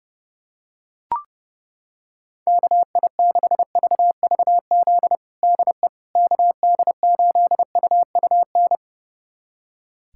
Morse code, heard as a keyed tone, about 30 wpm, 700 Hz. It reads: KI6VVZ DE KD8UUN
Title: USA Callsign DE Callsign - 6 Characters - 30wpm